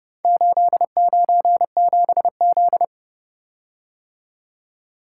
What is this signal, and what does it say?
Morse code, heard as a keyed tone, about 30 wpm, 700 Hz. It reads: 897Z